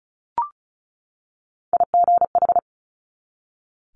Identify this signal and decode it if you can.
Morse code, heard as a keyed tone, about 36 wpm, 700 Hz. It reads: IGH
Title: Multi-letter Phonemes 35wpm